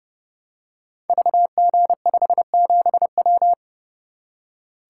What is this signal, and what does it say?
Morse code, heard as a keyed tone, about 31 wpm, 700 Hz. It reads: VG57W
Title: Code Groups - 5 Letter:Numbers - 30wpm